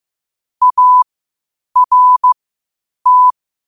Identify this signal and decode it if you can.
Morse code, heard as a keyed tone, about 15 wpm, spaced as for 8 wpm, 1000 Hz. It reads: ART
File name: qotdfiles / 8wpm